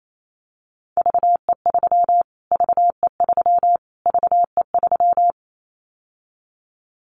Morse code, heard as a keyed tone, about 28 wpm, 700 Hz. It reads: VE3 VE3 VE3
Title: Callsign Prefixes for Top 25 DXCC - Repeated 3x - 28wpm